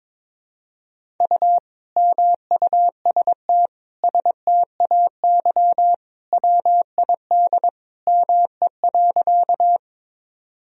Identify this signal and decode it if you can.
Morse code, heard as a keyed tone, about 23 wpm, 700 Hz. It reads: U MUST STAY WID ME.
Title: Sentences from Top 500 Words and Abbreviations 22wpm